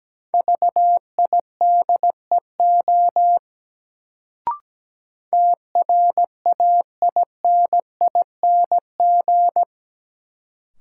Morse code, heard as a keyed tone, about 17 wpm, 700 Hz. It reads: VIDEO TRAINING
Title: Sentences from Top 1000 Words - Review Part 4 - New Word Drill - 17wpm